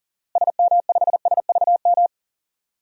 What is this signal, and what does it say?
Morse code, heard as a keyed tone, about 37 wpm, 700 Hz. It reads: SM5SVM
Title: DXCC-100 Call Signs - 6 Characters - Repeated 2x - 40wpm